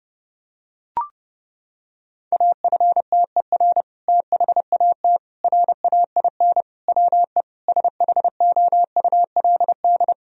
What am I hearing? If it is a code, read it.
Morse code, heard as a keyed tone, about 32 wpm, 700 Hz. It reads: AFTER THAT RAIN WE SHOULD
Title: Sentences from Top 1000 Words - Review Part 1 - 30wpm